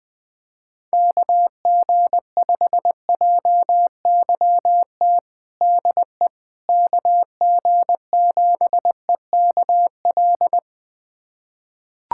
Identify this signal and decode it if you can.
Morse code, heard as a keyed tone, about 20 wpm, 700 Hz. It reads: KG5JYT DE KG7EKL
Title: USA Callsign DE Callsign 20wpm